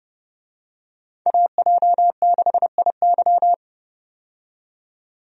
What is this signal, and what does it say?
Morse code, heard as a keyed tone, about 29 wpm, 700 Hz. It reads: AJ6IY
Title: USA Call Signs - 5 Characters - 30wpm